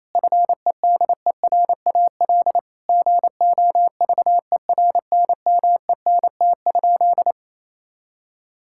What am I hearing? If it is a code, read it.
Morse code, heard as a keyed tone, about 27 wpm, 700 Hz. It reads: FEDERAL GOVERNMENT?
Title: Sentences from Top 1000 Words - Review Part 5 - 28wpm